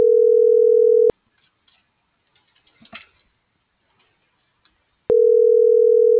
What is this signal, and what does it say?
Ringback tone (North America), 440 + 480 Hz.